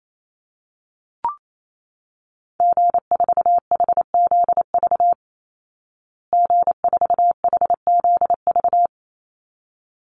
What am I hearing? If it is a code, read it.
Morse code, heard as a keyed tone, about 28 wpm, 700 Hz. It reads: G4HZV G4HZV
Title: Call Signs Repeated 3x - 28wpm